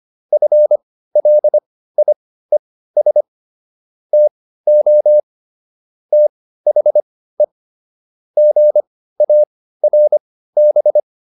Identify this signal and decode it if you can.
Morse code, heard as a keyed tone, about 25 wpm, spaced as for 15 wpm, 600 Hz. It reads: FLIES TO THE GARB